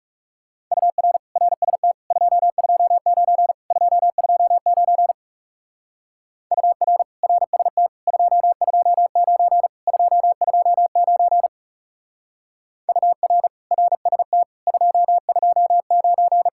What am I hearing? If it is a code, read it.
Morse code, heard as a keyed tone, about 36 wpm, 700 Hz. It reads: UR RST 229 229 UR RST 229 229 UR RST 229
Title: Speed-Racing Common QSO Phrases - 30wpm